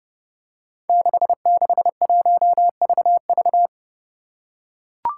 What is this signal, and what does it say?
Morse code, heard as a keyed tone, about 32 wpm, 700 Hz. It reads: 661VV
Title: Five Challenging Letter-Number Combinations 30wpm